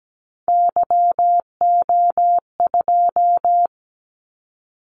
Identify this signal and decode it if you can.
Morse code, heard as a keyed tone, about 17 wpm, 700 Hz. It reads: YO2
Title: Callsign Prefixes for Top 25 DXCC - 17wpm